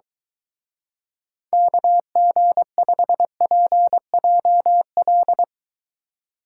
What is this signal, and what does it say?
Morse code, heard as a keyed tone, about 23 wpm, 700 Hz. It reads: KG5PJL